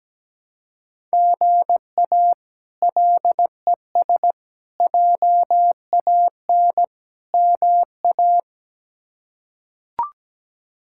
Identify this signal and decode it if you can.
Morse code, heard as a keyed tone, about 17 wpm, 700 Hz. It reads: GA LES JAN MA
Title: K1USN SST QSO Practice - 17wpm